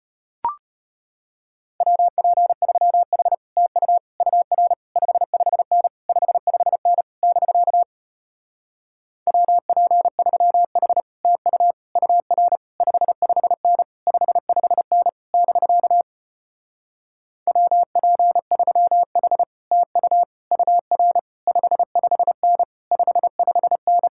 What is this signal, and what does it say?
Morse code, heard as a keyed tone, about 33 wpm, 700 Hz. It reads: WP3H TU UR 55N 55N <BK> WP3H TU UR 55N 55N <BK> WP3H TU UR 55N 55N